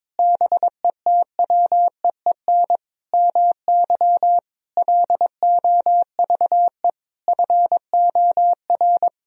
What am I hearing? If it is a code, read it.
Morse code, heard as a keyed tone, about 22 wpm, 700 Hz. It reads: BETWEEN MY LOVE FOR
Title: Sentences from Top 1000 Words - Review Part 1 - 22wpm